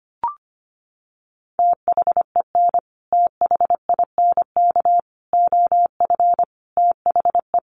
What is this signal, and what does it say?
Morse code, heard as a keyed tone, about 25 wpm, 700 Hz. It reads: THEN THINK OF THE
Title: Sentences from Top 1000 Words 25wpm